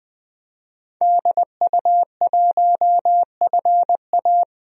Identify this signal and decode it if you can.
Morse code, heard as a keyed tone, about 21 wpm, 700 Hz. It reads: DU1FA
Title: DXCC-100 Call Signs - 5 Characters - 20wpm